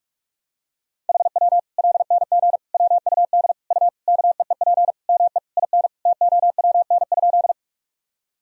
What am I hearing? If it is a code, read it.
Morse code, heard as a keyed tone, about 41 wpm, 700 Hz. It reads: HW LNG WUD U KEEP ME IN TOWN?
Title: Sentences from Top 500 Words and Abbreviations 45wpm